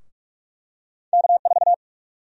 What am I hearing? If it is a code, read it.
Morse code, heard as a keyed tone, about 46 wpm, 700 Hz. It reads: K4
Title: USA Call Sign Prefixes - Repeated 2x - 45wpm